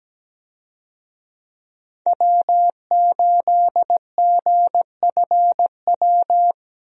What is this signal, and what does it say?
Morse code, heard as a keyed tone, about 17 wpm, 700 Hz. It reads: W8GFW
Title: USA Call Signs - 5 Characters - 17wpm